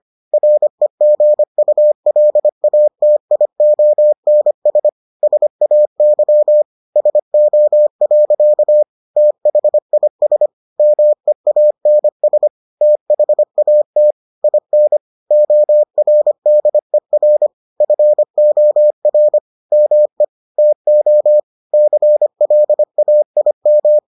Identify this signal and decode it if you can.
Morse code, heard as a keyed tone, about 25 wpm, 600 Hz. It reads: REGULATIONS SAY SO. THIS MEANS THAT IN ORDER FOR ME TO CLAIM